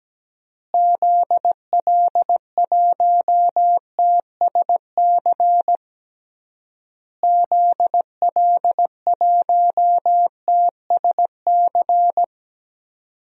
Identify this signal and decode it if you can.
Morse code, heard as a keyed tone, about 17 wpm, 700 Hz. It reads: ZL1TSC ZL1TSC
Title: DXCC-100 Call Signs - 6 Characters - Repeated 3x - 17wpm